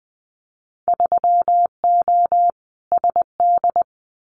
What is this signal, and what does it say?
Morse code, heard as a keyed tone, about 20 wpm, 700 Hz. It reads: 3O SD
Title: Winter Field Day Exchanges 20wpm